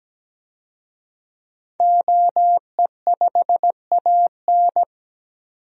Morse code, heard as a keyed tone, about 17 wpm, 700 Hz. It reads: OE5AN
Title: DXCC-100 Call Signs - 5 Characters - 17wpm